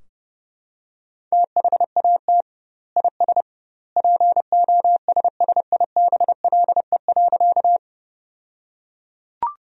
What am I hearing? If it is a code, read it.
Morse code, heard as a keyed tone, about 30 wpm, 700 Hz. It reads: THAT IS POSSIBLE.
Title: 2x Word-Spacing - Sentences from Top 500 Words - 30wpm